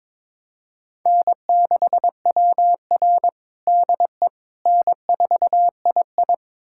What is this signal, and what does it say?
Morse code, heard as a keyed tone, about 23 wpm, 700 Hz. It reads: N6WR DE N4II
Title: USA Callsign DE Callsign - 4 Characters - 22wpm